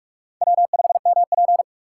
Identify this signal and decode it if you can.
Morse code, heard as a keyed tone, about 46 wpm, 700 Hz. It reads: W5MP
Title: USA Call Signs - 4 Characters - 6x Character Spacing - 45wpm